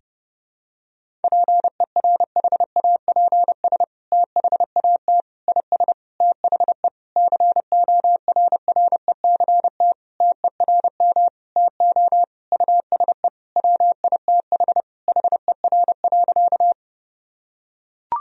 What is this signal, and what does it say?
Morse code, heard as a keyed tone, about 30 wpm, 700 Hz. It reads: PERHAPS THAT IS THE CORRECT TERM TO USE WITH HER.
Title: Sentences from Top 1000 Words - Review Part 4 - 30wpm